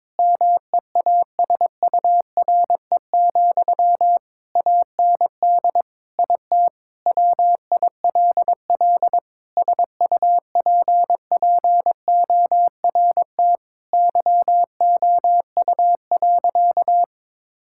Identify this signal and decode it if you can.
Morse code, heard as a keyed tone, about 22 wpm, 700 Hz. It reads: MEASURE, AND IT WILL SUPPORT YOU.
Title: Sentences from Top 1000 Words - Review Part 5 - 22wpm